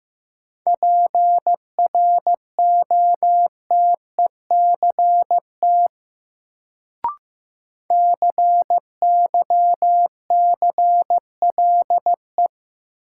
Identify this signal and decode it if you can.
Morse code, heard as a keyed tone, about 15 wpm, 700 Hz. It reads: PROTECT CYCLE
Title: Sentences from Top 900 Words - New Word Drill - 15wpm